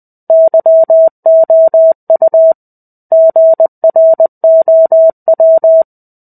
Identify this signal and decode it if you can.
Morse code, heard as a keyed tone, about 20 wpm, 650 Hz. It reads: YOU GROW
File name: 05-Chapter-05 20 WPM